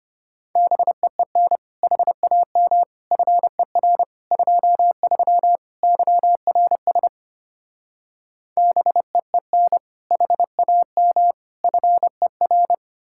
Morse code, heard as a keyed tone, about 33 wpm, 700 Hz. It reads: BEEN HAM FER 23 YRS BEEN HAM FER